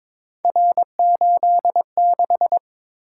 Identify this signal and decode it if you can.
Morse code, heard as a keyed tone, about 23 wpm, 700 Hz. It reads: R86
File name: practice / 22wpm / Three Letter-Number Combinations - 22wpm